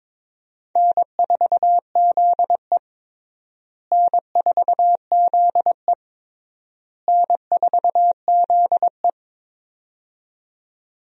Morse code, heard as a keyed tone, about 22 wpm, 700 Hz. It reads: N4ZE N4ZE N4ZE
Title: USA Call Signs - 4 Characters - Repeated 3x - 22wpm